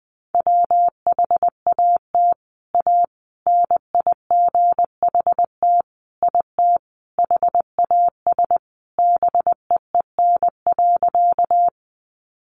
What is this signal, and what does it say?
Morse code, heard as a keyed tone, about 20 wpm, 700 Hz. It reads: WHAT A NIGHT IT HAS BEEN.
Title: Sentences from Top 300 Words 20wpm